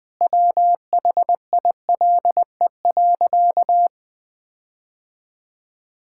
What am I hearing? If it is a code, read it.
Morse code, heard as a keyed tone, about 20 wpm, 700 Hz. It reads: WHILE.
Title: Sentences with 4-Word Phrases - 20wpm